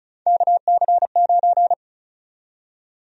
Morse code, heard as a keyed tone, about 36 wpm, 700 Hz. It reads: KC9
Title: USA Call Sign Prefixes - Repeated 2x - 35wpm